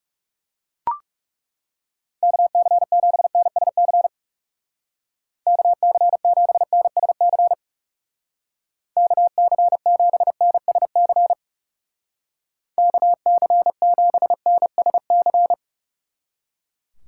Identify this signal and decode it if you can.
Morse code, heard as a keyed tone, about 37 wpm, 700 Hz. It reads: KC7NSC KC7NSC KC7NSC KC7NSC
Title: Speed Racing - USA 2x3 Call Signs 30wpm